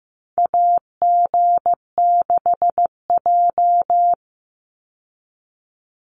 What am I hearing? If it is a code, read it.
Morse code, heard as a keyed tone, about 15 wpm, 700 Hz. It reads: AG6J
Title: USA Call Signs 15wpm